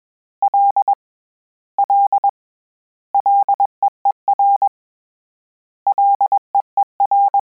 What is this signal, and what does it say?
Morse code, heard as a keyed tone, about 21 wpm, 800 Hz. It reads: L L LEER LEER
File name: Combinations / Combinations-mp3s / Combinations@20